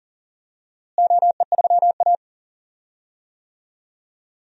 Morse code, heard as a keyed tone, about 40 wpm, 700 Hz. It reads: OE3A
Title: DXCC-100 Call Signs - 4 Characters - Repeated 3x - 40wpm